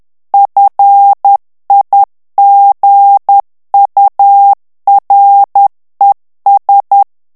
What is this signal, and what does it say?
Morse code, heard as a keyed tone, about 10 wpm, 800 Hz. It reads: FIGURES
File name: morse mp3 / ANGLAIS texte clair 2 vitesse 10 mots minute